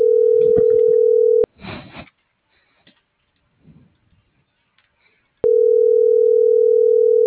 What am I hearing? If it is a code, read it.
Ringback tone (North America), 440 + 480 Hz.